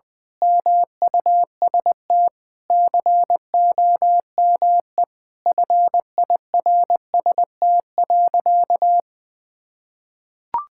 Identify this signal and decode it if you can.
Morse code, heard as a keyed tone, about 20 wpm, 700 Hz. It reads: MUST COME FIRST.
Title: Sentences from Top 1000 Words - Review Part 6 - 20wpm